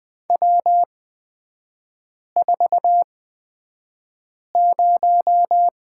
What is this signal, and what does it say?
Morse code, heard as a keyed tone, about 20 wpm, spaced as for 5 wpm, 700 Hz. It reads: W40
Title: 236 Lesson 38e - Letter q - Three Characters 20wpm